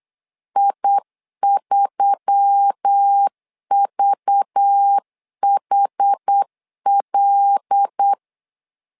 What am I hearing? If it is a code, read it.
Morse code, heard as a keyed tone, about 8 wpm, 800 Hz. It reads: I3VHL
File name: LEZIONI / 5 LBVC / 5 LBVC 8wpm 2935